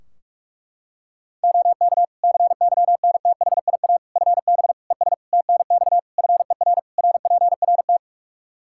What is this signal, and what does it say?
Morse code, heard as a keyed tone, about 46 wpm, 700 Hz. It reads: OK CYNTHIA FB ES TNX FER RPRT
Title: K1USN SST QSO Practice 45wpm